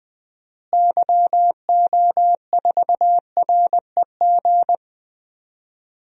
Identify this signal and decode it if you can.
Morse code, heard as a keyed tone, about 20 wpm, 700 Hz. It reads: YO4REG
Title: DXCC-100 Call Signs - 6 Characters - Repeated 3x - 20wpm